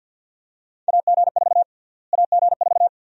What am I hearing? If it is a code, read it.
Morse code, heard as a keyed tone, about 53 wpm, 700 Hz. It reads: AG4 AG4
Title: USA Call Sign Prefixes - Repeated 2x - 50wpm